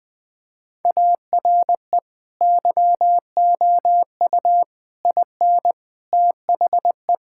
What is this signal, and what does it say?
Morse code, heard as a keyed tone, about 21 wpm, 700 Hz. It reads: ARE YOU IN THE
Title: Sentences from Top 1000 Words - Review Part 5 - 20wpm